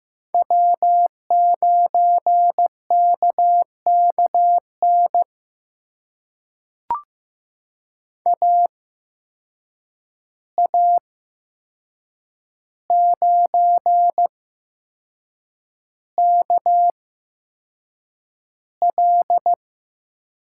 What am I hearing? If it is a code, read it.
Morse code, heard as a keyed tone, about 15 wpm, spaced as for 6 wpm, 700 Hz. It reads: W9KKN AA9KL
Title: USA Call Signs - 5 Characters - 8x Character Spacing - 15wpm